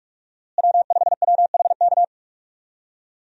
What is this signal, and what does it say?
Morse code, heard as a keyed tone, about 46 wpm, 700 Hz. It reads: W5WHK
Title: USA Call Signs - 5 Characters - Repeated 3x - 45wpm